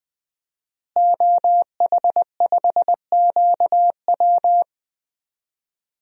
Morse code, heard as a keyed tone, about 20 wpm, 700 Hz. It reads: OH5QW
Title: DXCC-100 Call Signs - 5 Characters - Repeated 2x - 20wpm